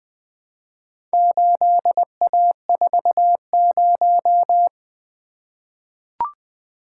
Morse code, heard as a keyed tone, about 20 wpm, 700 Hz. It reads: 8A40
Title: Four Letter-Number Combinations - 20wpm